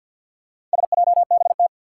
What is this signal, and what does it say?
Morse code, heard as a keyed tone, about 51 wpm, 700 Hz. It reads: SJBT
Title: Four Challenging Letter-Number Combinations 50wpm